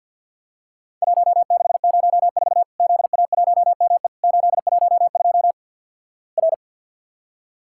Morse code, heard as a keyed tone, about 50 wpm, 700 Hz. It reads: 1604 ZA1ME 812 R
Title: Worked All Europe DX Contest - QTC Exchange with Repeats - 50wpm